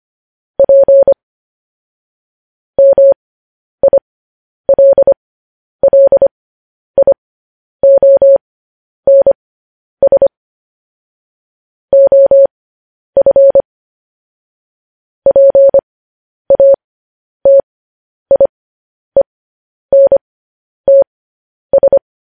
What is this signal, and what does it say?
Morse code, heard as a keyed tone, about 25 wpm, spaced as for 10 wpm, 550 Hz. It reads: P MILLIONS OF PATIENTS
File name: podcast / News headlines-1x-25x10